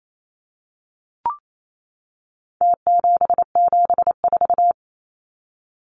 Morse code, heard as a keyed tone, about 28 wpm, 700 Hz. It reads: T774
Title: Numbers - Contest Style - 4-Digit 28wpm